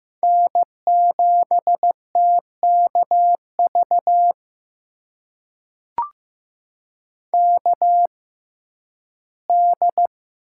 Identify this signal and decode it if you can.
Morse code, heard as a keyed tone, about 15 wpm, spaced as for 11 wpm, 700 Hz. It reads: N7TKV KD
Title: USA Call Signs - 5 Characters - 6x Character Spacing - 15wpm